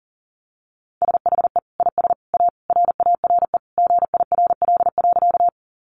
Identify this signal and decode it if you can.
Morse code, heard as a keyed tone, about 40 wpm, 700 Hz. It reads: SHE IS A RARE GIRL.
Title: Sentences from Top 1100 Words 40wpm